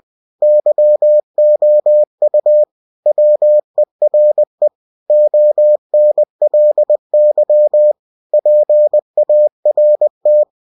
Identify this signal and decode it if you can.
Morse code, heard as a keyed tone, about 20 wpm, 600 Hz. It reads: YOU WERE ONLY PART
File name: cw-books / winnie-the-pooh / winnie-20wpm-page-0005